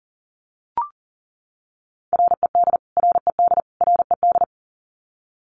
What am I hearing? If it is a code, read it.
Morse code, heard as a keyed tone, about 40 wpm, 700 Hz. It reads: RED RED RED
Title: Top 500 Words Repeated 40wpm